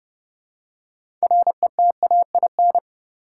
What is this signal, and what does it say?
Morse code, heard as a keyed tone, about 30 wpm, 700 Hz. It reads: RETAIN
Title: Six Letter Words - 30wpm